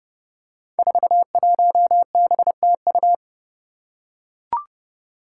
Morse code, heard as a keyed tone, about 30 wpm, 700 Hz. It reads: 41BTU
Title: Five Challenging Letter-Number Combinations 30wpm